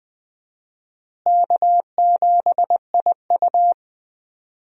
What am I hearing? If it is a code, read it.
Morse code, heard as a keyed tone, about 21 wpm, 700 Hz. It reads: K7IU
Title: USA Call Signs - 4 Characters - Repeated 2x - 20wpm